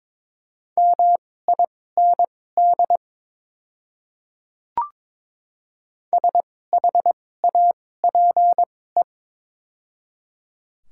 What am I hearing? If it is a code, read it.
Morse code, heard as a keyed tone, about 22 wpm, spaced as for 16 wpm, 690 Hz. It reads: MIND SHAPE
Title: 2x Character Spacing - Top 500 Words - 22wpm